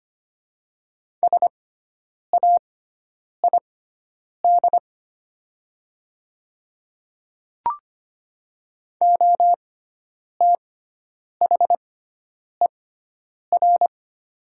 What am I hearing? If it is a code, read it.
Morse code, heard as a keyed tone, about 25 wpm, spaced as for 9 wpm, 700 Hz. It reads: SAID OTHER
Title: 6x Character Spacing - Top 100 Words - 25wpm